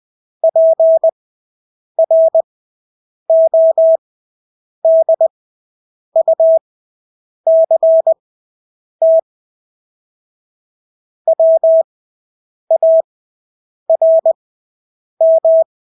Morse code, heard as a keyed tone, about 20 wpm, spaced as for 8 wpm, 650 Hz. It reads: PRODUCT WARM
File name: Engelske ord 20/8 wpm